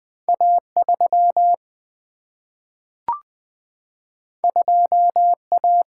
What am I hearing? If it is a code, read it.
Morse code, heard as a keyed tone, about 20 wpm, 700 Hz. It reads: A 3 2 A
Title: Single-Letter Single-Number 20wpm